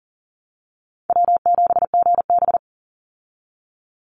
Morse code, heard as a keyed tone, about 40 wpm, 700 Hz. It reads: W7GB
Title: Easy Call Signs Repeated 3x - 40wpm